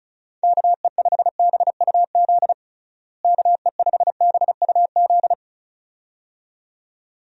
Morse code, heard as a keyed tone, about 36 wpm, 700 Hz. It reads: KE5BUZ KE5BUZ
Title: USA Call Signs - 6 Characters - Repeated 2x - 35wpm